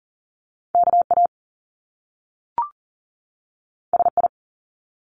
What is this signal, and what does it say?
Morse code, heard as a keyed tone, about 39 wpm, 700 Hz. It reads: KA SI
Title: Top Two Letter Combinations in Words 40wpm